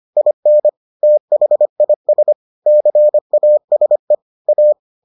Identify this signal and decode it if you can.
Morse code, heard as a keyed tone, about 25 wpm, 600 Hz. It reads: IN THIS CASE A